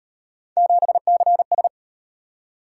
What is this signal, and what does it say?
Morse code, heard as a keyed tone, about 36 wpm, 700 Hz. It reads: 7CS